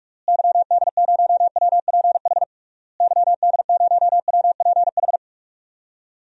Morse code, heard as a keyed tone, about 46 wpm, 700 Hz. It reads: YD0WPH YD0WPH
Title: DXCC-100 Call Signs - 6 Characters - Repeated 3x - 45wpm